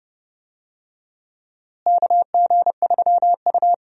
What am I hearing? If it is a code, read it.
Morse code, heard as a keyed tone, about 30 wpm, 700 Hz. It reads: KG3U
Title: USA Call Signs - 4 Characters - Repeated 2x - 30wpm